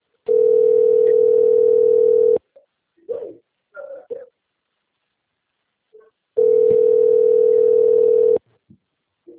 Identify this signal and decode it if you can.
Ringback tone (North America), 440 + 480 Hz.